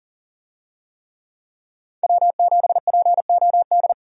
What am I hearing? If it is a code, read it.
Morse code, heard as a keyed tone, about 40 wpm, 700 Hz. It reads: W7POD
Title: USA Call Signs - 5 Characters - 40wpm